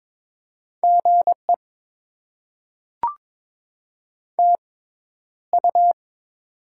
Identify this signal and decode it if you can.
Morse code, heard as a keyed tone, about 22 wpm, spaced as for 11 wpm, 700 Hz. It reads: GE TU
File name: Top 20 QSO Elements - 6x Character Spacing - 22wpm